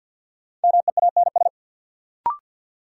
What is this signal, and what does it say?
Morse code, heard as a keyed tone, about 53 wpm, 700 Hz. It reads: MEANS
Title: Sets of 5 Words - New Word Drill - 50wpm